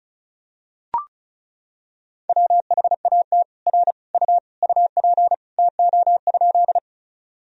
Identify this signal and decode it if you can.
Morse code, heard as a keyed tone, about 36 wpm, 700 Hz. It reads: WHAT R U UP TO?
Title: Sentences from Top 100 Words and Abbreviations 35wpm